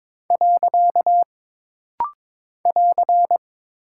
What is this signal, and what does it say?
Morse code, heard as a keyed tone, about 22 wpm, 700 Hz. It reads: . <AR>
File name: practice / 22wpm / Single Punctuation-Prosign - Rapid-Fire - 22wpm